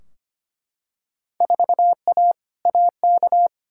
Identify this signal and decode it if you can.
Morse code, heard as a keyed tone, about 25 wpm, 700 Hz. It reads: 4A AK
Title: Field Day Exchanges - 25wpm